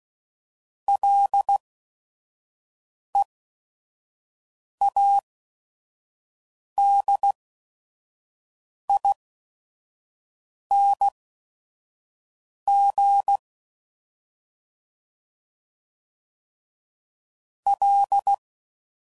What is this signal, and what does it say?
Morse code, heard as a keyed tone, about 16 wpm, spaced as for 5 wpm, 800 Hz. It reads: LEADING L
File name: Prefixes / FLs & THs / Single L's